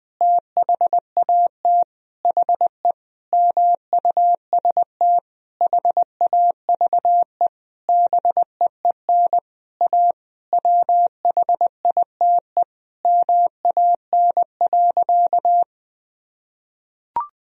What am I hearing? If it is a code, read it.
Morse code, heard as a keyed tone, about 20 wpm, 700 Hz. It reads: THAT HE MUST HAVE BEEN A WHITE MAN.
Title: Sentences from Top 1000 Words - Review Part 1 - 20wpm